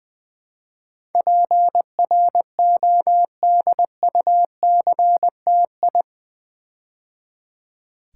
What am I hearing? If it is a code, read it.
Morse code, heard as a keyed tone, about 20 wpm, 700 Hz. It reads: PRODUCTI
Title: Build A Word - 4 Letters At A Time - 10-Letter Words - 20wpm